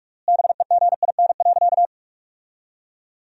Morse code, heard as a keyed tone, about 41 wpm, 700 Hz. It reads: BEGIN.
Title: 3x Word-Spacing - Sentences from Top 300 Words - 45wpm